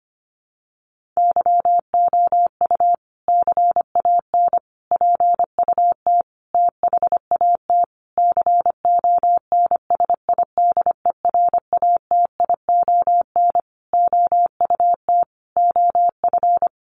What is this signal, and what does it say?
Morse code, heard as a keyed tone, about 25 wpm, 700 Hz. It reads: YOU CAN PUT THAT CONSIDERATION OUT OF
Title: Sentences from Top 1500 Words 25wpm